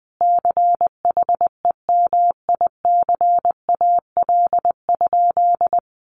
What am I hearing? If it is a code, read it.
Morse code, heard as a keyed tone, about 20 wpm, 700 Hz. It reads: CHEMICAL?
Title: Sentences from Top 1200 Words 20wpm